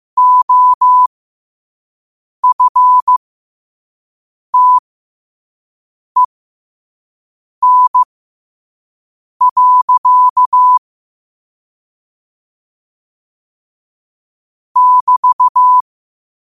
Morse code, heard as a keyed tone, about 15 wpm, spaced as for 5 wpm, 1000 Hz. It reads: OFTEN. <BT>
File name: qotdfiles / 5wpm